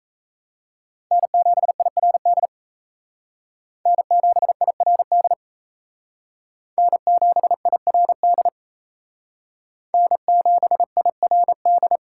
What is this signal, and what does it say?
Morse code, heard as a keyed tone, about 37 wpm, 700 Hz. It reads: N7IRD N7IRD N7IRD N7IRD